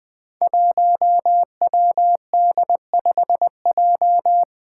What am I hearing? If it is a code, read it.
Morse code, heard as a keyed tone, about 20 wpm, 700 Hz. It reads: 1WD5J
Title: Five Challenging Letter-Number Combinations 20wpm